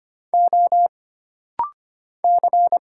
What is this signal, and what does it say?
Morse code, heard as a keyed tone, about 25 wpm, 700 Hz. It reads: O C
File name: Single Character (Letter:Number:Punctuation:Prosign) - Rapid-Fire - 25wpm